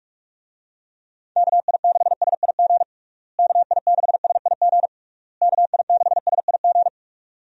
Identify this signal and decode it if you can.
Morse code, heard as a keyed tone, about 48 wpm, 700 Hz. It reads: KI6SIG KI6SIG KI6SIG
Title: USA Call Signs - 6 Characters - Repeated 3x - 45wpm